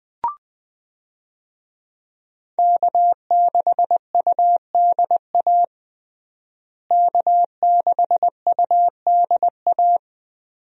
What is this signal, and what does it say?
Morse code, heard as a keyed tone, about 20 wpm, 700 Hz. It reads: K6UDA K6UDA
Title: USA Call Signs - 5 Characters - Repeated 2x - 20wpm